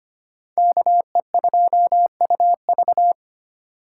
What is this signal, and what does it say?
Morse code, heard as a keyed tone, about 25 wpm, 700 Hz. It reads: KE2UV
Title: USA Call Signs - 5 Characters - 3x Character Spacing - 25wpm